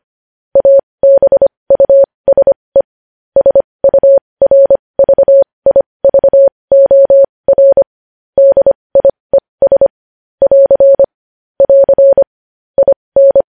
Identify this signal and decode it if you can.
Morse code, heard as a keyed tone, about 25 wpm, spaced as for 20 wpm, 550 Hz. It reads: ABUSE SURVIVOR DIES <AR> <AR> IN
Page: news headlines in morse code at 20 wpm